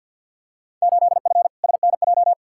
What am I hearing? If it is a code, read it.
Morse code, heard as a keyed tone, about 52 wpm, 700 Hz. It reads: 8F SNJ
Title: Field Day Exchanges Encore - 50wpm